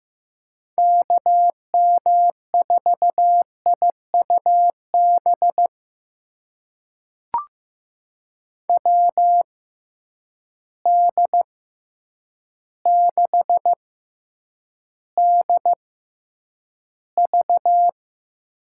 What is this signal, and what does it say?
Morse code, heard as a keyed tone, about 16 wpm, spaced as for 8 wpm, 700 Hz. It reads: KM4IUB WD6DV
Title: USA Call Signs - 6 Characters - 6x Character Spacing - 15wpm